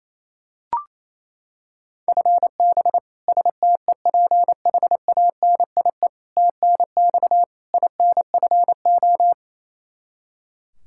Morse code, heard as a keyed tone, about 28 wpm, 700 Hz. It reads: FB STEPHANIE TNX INFO
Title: Practice QSO - Exchange 3 - More Info - 28wpm